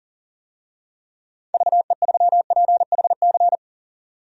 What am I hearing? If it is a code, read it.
Morse code, heard as a keyed tone, about 40 wpm, 700 Hz. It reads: VE3PHC
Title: DXCC-100 Call Signs - 6 Characters - 40wpm